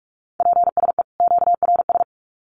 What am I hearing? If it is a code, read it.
Morse code, heard as a keyed tone, about 44 wpm, 700 Hz. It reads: PSE QRS
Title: Common QSO Phrases 45wpm